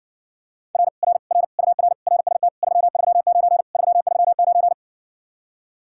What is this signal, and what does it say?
Morse code, heard as a keyed tone, about 60 wpm, 700 Hz. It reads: R R R UR RST 339 339